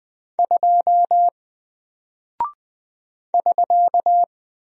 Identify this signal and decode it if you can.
Morse code, heard as a keyed tone, about 20 wpm, 700 Hz. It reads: 2 <SK>
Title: Single Character (Letter:Number:Punctuation:Prosign) Alt 20wpm